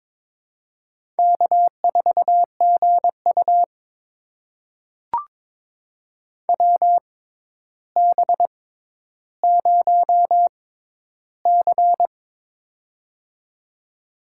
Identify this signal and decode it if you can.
Morse code, heard as a keyed tone, about 23 wpm, spaced as for 11 wpm, 700 Hz. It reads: K4GU WB0C
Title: USA Call Signs - 6x Character Spacing - 22wpm